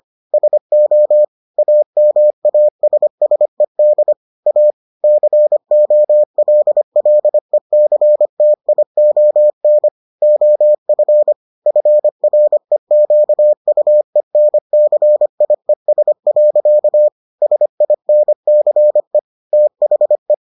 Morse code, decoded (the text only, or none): SO AMASSED A COLLECTION OF FREQUENCIES. SINCE THE